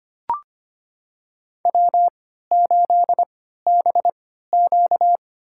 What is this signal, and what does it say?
Morse code, heard as a keyed tone, about 25 wpm, spaced as for 14 wpm, 700 Hz. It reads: W8BQ
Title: USA Call Signs - 3x Character Spacing - 25wpm